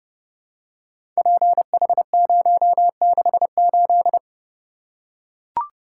Morse code, heard as a keyed tone, about 30 wpm, 700 Hz. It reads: PH068
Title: Code Groups - 5 Letter:Numbers - 30wpm